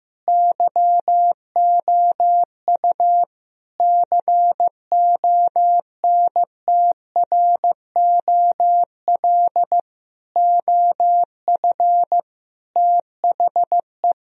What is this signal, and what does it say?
Morse code, heard as a keyed tone, about 15 wpm, 700 Hz. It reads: YOU CONTROL OF THE
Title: Sentences from Top 1000 Words - Review Part 3 - 15wpm